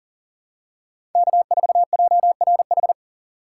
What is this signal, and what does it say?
Morse code, heard as a keyed tone, about 38 wpm, 700 Hz. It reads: K4JRH
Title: USA Call Signs - 5 Characters - 3x Character Spacing - 40wpm